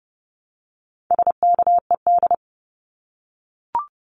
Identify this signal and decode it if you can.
Morse code, heard as a keyed tone, about 30 wpm, 700 Hz. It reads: SKED
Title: Abbreviations Repeated 3x - 30wpm